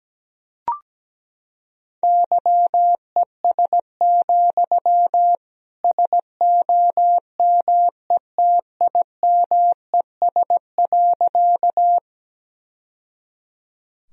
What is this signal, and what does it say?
Morse code, heard as a keyed tone, about 17 wpm, 700 Hz. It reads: YES, SOMETIMES.
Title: Sentences from Top 1000 Words - Review Part 3 - 17wpm